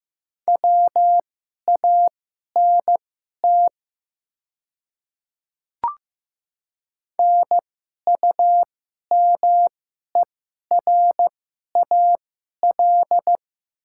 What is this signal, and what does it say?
Morse code, heard as a keyed tone, about 15 wpm, spaced as for 11 wpm, 700 Hz. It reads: WANT NUMERAL
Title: 2x Character Spacing - Top 500 Words - 15wpm